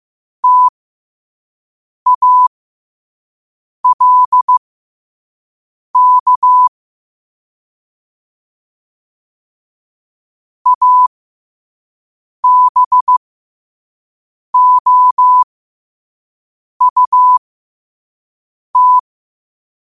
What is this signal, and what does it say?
Morse code, heard as a keyed tone, about 15 wpm, spaced as for 5 wpm, 1000 Hz. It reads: TALK ABOUT
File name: qotdfiles / 5wpm